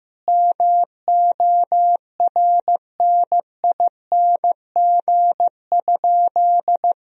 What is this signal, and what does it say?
Morse code, heard as a keyed tone, about 15 wpm, 700 Hz. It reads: MORNING?
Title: Sentences from Top 1000 Words - Review Part 2 - 15wpm